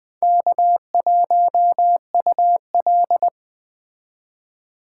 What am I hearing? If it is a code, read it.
Morse code, heard as a keyed tone, about 21 wpm, 700 Hz. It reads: K1UL
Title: Speed Racing - USA 1x2 Call Signs 17wpm